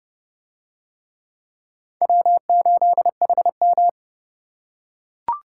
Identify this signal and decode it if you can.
Morse code, heard as a keyed tone, about 27 wpm, 700 Hz. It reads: W8HM